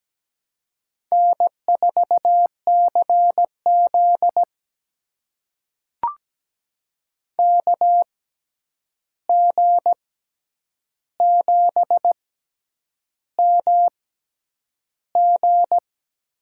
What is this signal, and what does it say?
Morse code, heard as a keyed tone, about 17 wpm, spaced as for 8 wpm, 700 Hz. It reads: N4CZ KG7MG
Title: USA Call Signs - 6x Character Spacing - 17wpm